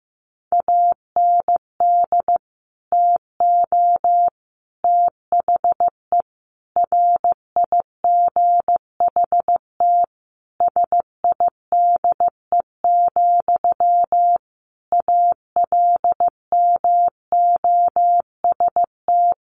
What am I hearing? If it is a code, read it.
Morse code, heard as a keyed tone, about 15 wpm, 700 Hz. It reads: AND TO THE RIGHT SIDE, ALMOST
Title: Sentences from Top 1300 Words 15wpm